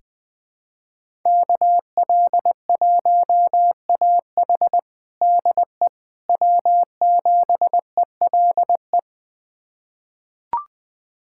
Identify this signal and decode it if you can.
Morse code, heard as a keyed tone, about 20 wpm, 700 Hz. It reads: KL1AH DE W7ELE
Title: USA Callsign DE Callsign 20wpm